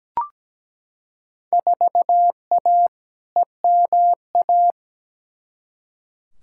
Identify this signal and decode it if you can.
Morse code, heard as a keyed tone, about 17 wpm, 700 Hz. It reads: 4A EMA
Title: Field Day Exchanges - 17wpm